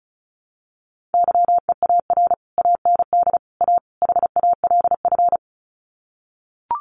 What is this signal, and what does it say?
Morse code, heard as a keyed tone, about 34 wpm, 700 Hz. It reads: YEAR AND A HALF
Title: Sets of 4 Words 35wpm